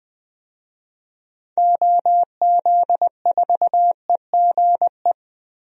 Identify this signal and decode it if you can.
Morse code, heard as a keyed tone, about 21 wpm, 700 Hz. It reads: OZ4EGE
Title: DXCC-100 Call Signs - 6 Characters - Repeated 3x - 20wpm